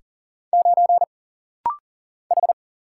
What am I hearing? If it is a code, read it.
Morse code, heard as a keyed tone, about 39 wpm, 700 Hz. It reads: 9 H
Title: Single Letter-Number - Rapid-Fire 40wpm